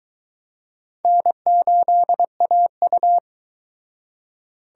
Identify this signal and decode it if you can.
Morse code, heard as a keyed tone, about 24 wpm, 700 Hz. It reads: N8AU